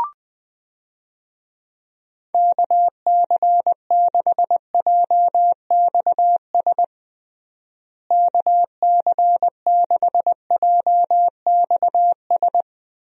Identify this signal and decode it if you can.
Morse code, heard as a keyed tone, about 20 wpm, 700 Hz. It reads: KC6JXS KC6JXS
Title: USA Call Signs - 6 Characters - Repeated 2x - 20wpm